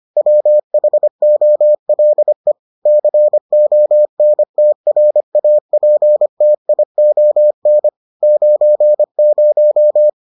Morse code, heard as a keyed tone, about 25 wpm, 600 Hz. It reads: WHOLE CONTRAPTION 90